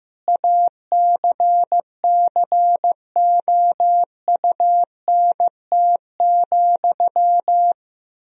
Morse code, heard as a keyed tone, about 15 wpm, 700 Hz. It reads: ACCOUNT,
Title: Sentences from Top 1000 Words - Review Part 4 - 15wpm